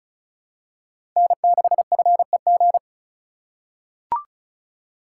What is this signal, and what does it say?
Morse code, heard as a keyed tone, about 36 wpm, 700 Hz. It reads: N6FEG
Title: USA Call Signs - 5 Characters - 4x Character Spacing - 35wpm